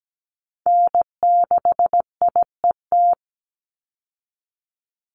Morse code, heard as a keyed tone, about 17 wpm, 700 Hz. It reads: N6IET
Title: Call Signs Repeated 2x - 17wpm